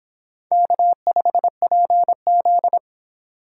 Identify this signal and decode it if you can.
Morse code, heard as a keyed tone, about 26 wpm, 700 Hz. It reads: K5PZ